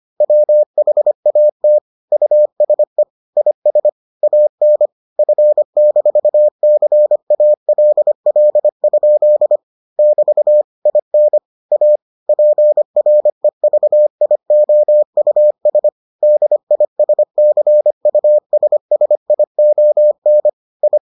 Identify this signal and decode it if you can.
Morse code, heard as a keyed tone, about 25 wpm, 600 Hz. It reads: WHAT USE IS AN F-CALL? <BT> IN A PREVIOUS DISCUSSION I